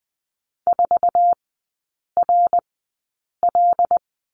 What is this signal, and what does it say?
Morse code, heard as a keyed tone, about 20 wpm, spaced as for 8 wpm, 690 Hz. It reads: 4RL
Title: 081 Lesson 14e - Number 2 - Three Characters 20wpm